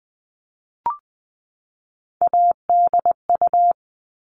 Morse code, heard as a keyed tone, about 20 wpm, 700 Hz. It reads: ADU
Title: All Cut Numbers - 3-Digit 20wpm